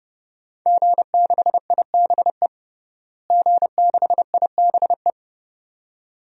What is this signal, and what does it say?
Morse code, heard as a keyed tone, about 30 wpm, 700 Hz. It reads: G6IBE G6IBE
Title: DXCC-100 Call Signs - 5 Characters - Repeated 2x - 30wpm